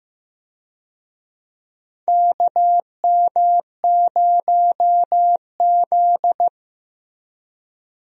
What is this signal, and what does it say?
Morse code, heard as a keyed tone, about 15 wpm, 700 Hz. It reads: KM0Z
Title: USA Call Signs - 4 Characters - Repeated 3x - 15wpm